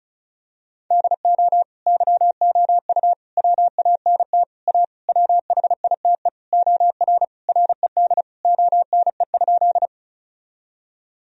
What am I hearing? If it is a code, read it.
Morse code, heard as a keyed tone, about 36 wpm, 700 Hz. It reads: DO YOU WANT A WHITE OR RED ONE?
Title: Sentences from Top 1000 Words - Review Part 2 - 35wpm